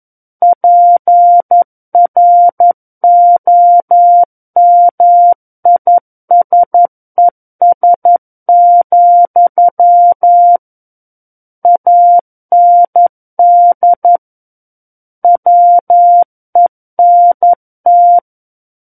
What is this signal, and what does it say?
Morse code, heard as a keyed tone, about 11 wpm, 700 Hz. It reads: PROMISES, AND WENT
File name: 11 WPM